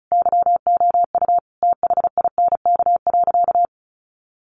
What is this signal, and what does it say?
Morse code, heard as a keyed tone, about 34 wpm, 700 Hz. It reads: YOU THINK.
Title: Sentences from Top 400 Words 35wpm